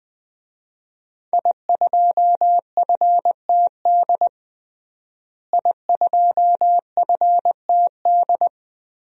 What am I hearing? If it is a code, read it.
Morse code, heard as a keyed tone, about 21 wpm, 700 Hz. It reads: I2FTD I2FTD
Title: DXCC-100 Call Signs - 5 Characters - Repeated 3x - 20wpm